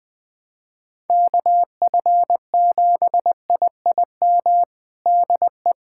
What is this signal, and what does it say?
Morse code, heard as a keyed tone, about 20 wpm, 700 Hz. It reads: KF7IIM DE
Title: USA Callsign DE Callsign - 6 Characters - 20wpm